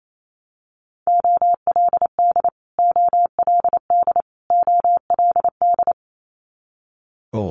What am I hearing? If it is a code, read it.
Morse code, heard as a keyed tone, about 28 wpm, 700 Hz. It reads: OLD OLD OLD
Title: Top 500 Words Repeated 28wpm